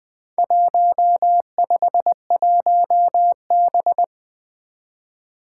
Morse code, heard as a keyed tone, about 21 wpm, 700 Hz. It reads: 151B
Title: Four Challenging Letter-Number Combinations 20wpm